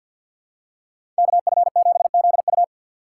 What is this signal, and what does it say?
Morse code, heard as a keyed tone, about 53 wpm, 700 Hz. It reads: KF7ZU
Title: USA Call Signs - 5 Characters - 2x Character Spacing - 50wpm